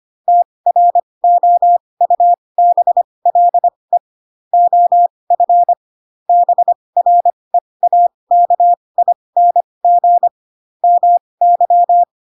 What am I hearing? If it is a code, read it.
Morse code, heard as a keyed tone, about 25 wpm, spaced as for 20 wpm, 700 Hz. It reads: TROUBLE OF BREAKING MY